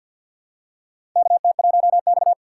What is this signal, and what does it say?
Morse code, heard as a keyed tone, about 51 wpm, 700 Hz. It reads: KT1X
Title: USA Call Signs - 4 Characters - 2x Character Spacing - 50wpm